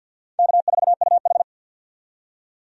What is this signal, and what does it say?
Morse code, heard as a keyed tone, about 48 wpm, 700 Hz. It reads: K4UH
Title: USA Call Signs - 4 Characters - 50wpm